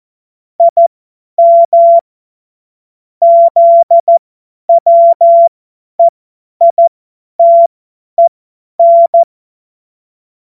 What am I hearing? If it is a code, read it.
Morse code, heard as a keyed tone, about 14 wpm, spaced as for 10 wpm, 680 Hz. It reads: IM ZWEITEN
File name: Morsekurs / Witze / Witz 05 März 2024 50 F70